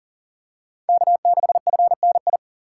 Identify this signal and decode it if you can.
Morse code, heard as a keyed tone, about 40 wpm, 700 Hz. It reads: K6FNI
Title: USA Call Signs - 5 Characters - 2x Character Spacing - 40wpm